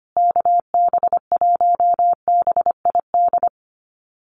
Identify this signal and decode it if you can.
Morse code, heard as a keyed tone, about 25 wpm, 700 Hz. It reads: KB1BID
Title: USA Call Signs Repeated 2x 25wpm